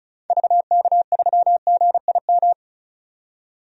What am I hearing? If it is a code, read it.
Morse code, heard as a keyed tone, about 36 wpm, 700 Hz. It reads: VK3GIM
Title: DXCC-100 Call Signs - 6 Characters - 35wpm